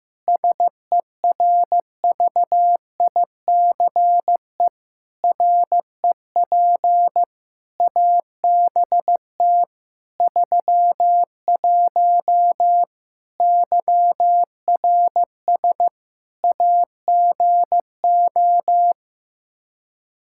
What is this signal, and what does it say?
Morse code, heard as a keyed tone, about 15 wpm, 700 Hz. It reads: SERVICE REP ABT 31 YRS AGO
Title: Practice QSO - Exchange 3 - More Info - 15wpm